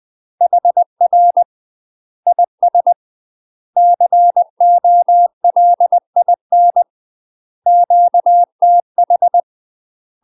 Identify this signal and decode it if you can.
Morse code, heard as a keyed tone, about 20 wpm, 700 Hz. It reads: HR IS COLIN QTH
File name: UK MASTER QSOs TRAINING / 20WPM / UK MASTER QSOs 0071